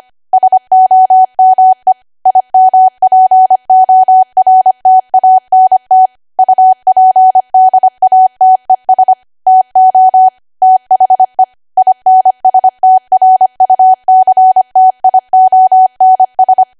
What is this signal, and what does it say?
Morse code, heard as a keyed tone, about 25 wpm, 750 Hz. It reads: SOME IMPORTANT UPDATES TO THE INSTRUCTIONS
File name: Archive / 25 WPM / 25WPM